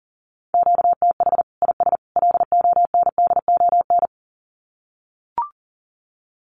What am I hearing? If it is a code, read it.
Morse code, heard as a keyed tone, about 40 wpm, 700 Hz. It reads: QTH IS LONDON
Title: Common QSO Phrases 40wpm